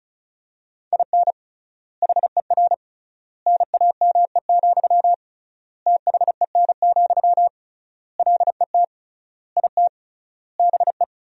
Morse code, heard as a keyed tone, about 36 wpm, 700 Hz. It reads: IN HER NAME, THEN, LET IT BE
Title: 3x Word-Spacing - Sentences from Top 500 Words - 35wpm